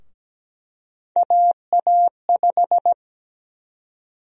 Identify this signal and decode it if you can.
Morse code, heard as a keyed tone, about 17 wpm, 700 Hz. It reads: AA5
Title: USA Call Sign Prefixes - Repeated 2x - 17wpm